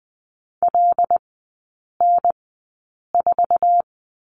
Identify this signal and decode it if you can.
Morse code, heard as a keyed tone, about 19 wpm, spaced as for 8 wpm, 700 Hz. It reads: LN4
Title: 074 Lesson 13e - Letter L - Three Characters 20wpm